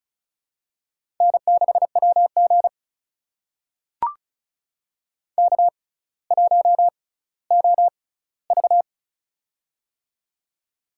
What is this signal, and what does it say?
Morse code, heard as a keyed tone, about 36 wpm, spaced as for 18 wpm, 700 Hz. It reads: N6WG K1OV
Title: USA Call Signs - 4 Characters - 6x Character Spacing - 35wpm